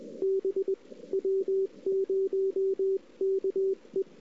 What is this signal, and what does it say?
Morse code, heard as a keyed tone, about 19 wpm, 380 Hz. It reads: B W0KE
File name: CW / 2008-ss-cw-k5tr-op-n5rz -05:00